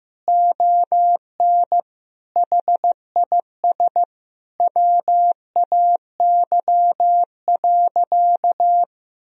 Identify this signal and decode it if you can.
Morse code, heard as a keyed tone, about 15 wpm, 700 Hz. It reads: ON HIS WAY.
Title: Sentences from Top 300 Words and Abbreviations 15wpm